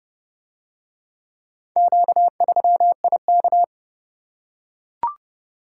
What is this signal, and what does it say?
Morse code, heard as a keyed tone, about 29 wpm, 700 Hz. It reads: Q3IK